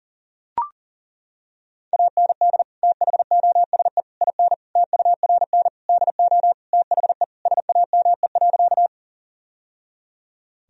Morse code, heard as a keyed tone, about 40 wpm, 700 Hz. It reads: AND THOSE IN TURN DO THE SAME.
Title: Sentences from Top 1000 Words - Review Part 1 - 40wpm